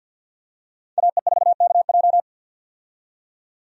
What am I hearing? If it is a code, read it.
Morse code, heard as a keyed tone, about 51 wpm, 700 Hz. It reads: AE4KJ
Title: USA Call Signs - 5 Characters - 50wpm